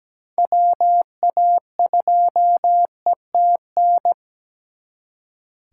Morse code, heard as a keyed tone, about 20 wpm, 700 Hz. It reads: WA2ETN
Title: Speed Racing - USA 2x3 Call Signs 17wpm